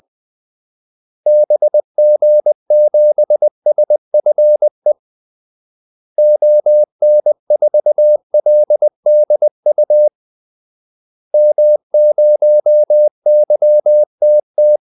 Morse code, heard as a keyed tone, about 20 wpm, 600 Hz. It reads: BG7SFE ON4LDU M0YTT
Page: Kaldesignaler 20 wpm